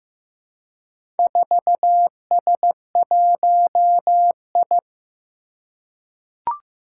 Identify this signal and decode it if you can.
Morse code, heard as a keyed tone, about 15 wpm, 700 Hz. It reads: 4S1I
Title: Four Challenging Letter-Number Combinations 15wpm